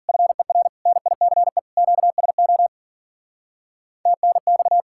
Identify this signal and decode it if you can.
Morse code, heard as a keyed tone, about 51 wpm, 700 Hz. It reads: FER NICE QSO TNX